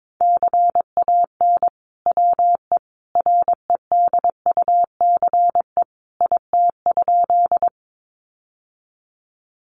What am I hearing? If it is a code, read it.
Morse code, heard as a keyed tone, about 22 wpm, 700 Hz. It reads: CAN WE REDUCE IT?
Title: Sentences from Top 1100 Words 22wpm